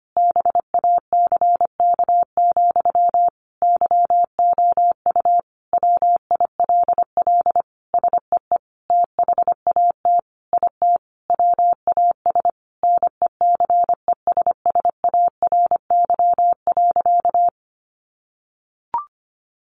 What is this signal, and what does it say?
Morse code, heard as a keyed tone, about 25 wpm, 700 Hz. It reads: BACK, YOU WILL SEE THAT IT WAS NECESSARY.
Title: Sentences from Top 800 Words 25wpm